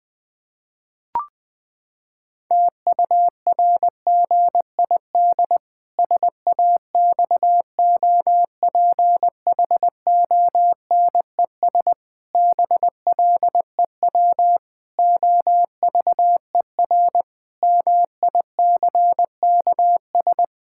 Morse code, heard as a keyed tone, about 20 wpm, 700 Hz. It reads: TURGID SAXOPHONES BLEW OVER MICKS
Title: Pangram Challenge v2 - 20wpm